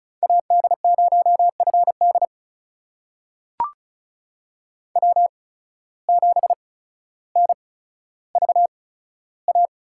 Morse code, heard as a keyed tone, about 36 wpm, spaced as for 15 wpm, 700 Hz. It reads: AD0FD W7NVA
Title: USA Call Signs - 8x Character Spacing - 35wpm